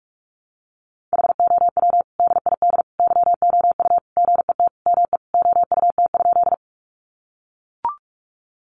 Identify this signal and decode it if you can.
Morse code, heard as a keyed tone, about 44 wpm, 700 Hz. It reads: HOW DID YOU GET ME OUT?
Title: Sentences from Top 200 Words 45wpm